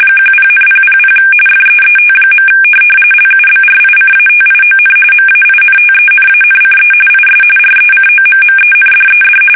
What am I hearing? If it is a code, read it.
RTTY, about 45.45 baud. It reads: 10,000 FEET. LUBRICATION MAINTENANCE IS RECOMMENDED
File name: fsk850 / teletype-corp